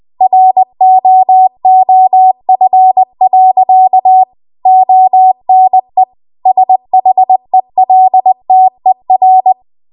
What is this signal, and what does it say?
Morse code, heard as a keyed tone, about 20 wpm, 750 Hz. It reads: ROOF. ONE SHELTER